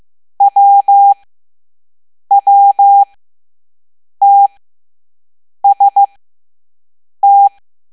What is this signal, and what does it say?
Morse code, heard as a keyed tone, about 15 wpm, spaced as for 6 wpm, 800 Hz. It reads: WWTST
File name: Morse / Level7-All-A